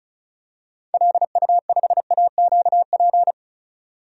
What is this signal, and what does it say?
Morse code, heard as a keyed tone, about 36 wpm, 700 Hz. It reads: LU5AQP
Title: DXCC-100 Call Signs - 6 Characters - 35wpm